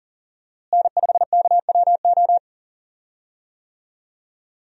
Morse code, heard as a keyed tone, about 40 wpm, 700 Hz. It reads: N5KWO
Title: USA Call Signs - 5 Characters - Repeated 3x - 40wpm